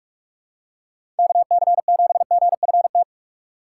Morse code, heard as a keyed tone, about 38 wpm, 700 Hz. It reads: KC7GFT